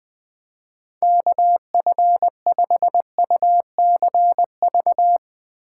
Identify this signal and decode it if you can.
Morse code, heard as a keyed tone, about 20 wpm, 700 Hz. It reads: KF5UCV
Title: Speed Racing - USA 2x3 Call Signs 20wpm